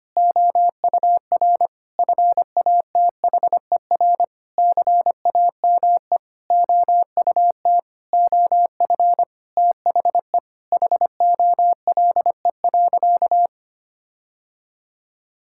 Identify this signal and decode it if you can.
Morse code, heard as a keyed tone, about 25 wpm, 700 Hz. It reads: OUR FATHER CAME OUT OF THE HOLE.
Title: Sentences from Top 1600 Words 25wpm